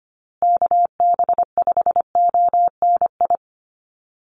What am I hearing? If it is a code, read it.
Morse code, heard as a keyed tone, about 25 wpm, 700 Hz. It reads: KB5ONI
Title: USA Call Signs Repeated 3x 25wpm